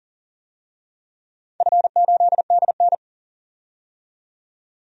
Morse code, heard as a keyed tone, about 40 wpm, 700 Hz. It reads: F8DN
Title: DXCC-100 Call Signs - 4 Characters - 40wpm